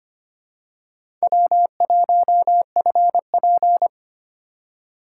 Morse code, heard as a keyed tone, about 25 wpm, 700 Hz. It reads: W1FP
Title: USA Call Signs - 2x Character Spacing - 25wpm